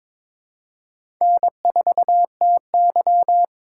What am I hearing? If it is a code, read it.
Morse code, heard as a keyed tone, about 22 wpm, 700 Hz. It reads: N4TY
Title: USA Call Signs - 4 Characters - 4x Character Spacing - 22wpm